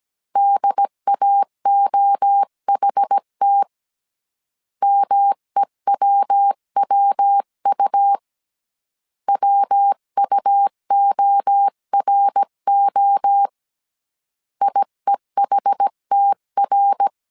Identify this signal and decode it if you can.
Morse code, heard as a keyed tone, about 17 wpm, 800 Hz. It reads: DAOHT MEWWU WUORO IEHTR
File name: CW / DET / 1 3 1500C S A 16wpm 2141